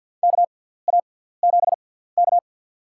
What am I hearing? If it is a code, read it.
Morse code, heard as a keyed tone, about 50 wpm, spaced as for 17 wpm, 690 Hz. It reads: KA7K
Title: USA Call Signs - 4 Characters - 6x Character Spacing - 50wpm